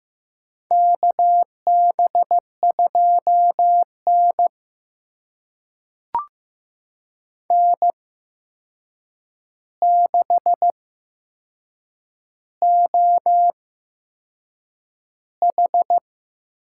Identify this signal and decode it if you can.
Morse code, heard as a keyed tone, about 15 wpm, spaced as for 6 wpm, 700 Hz. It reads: KB2N N6OH
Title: USA Call Signs - 4 Characters - 8x Character Spacing - 15wpm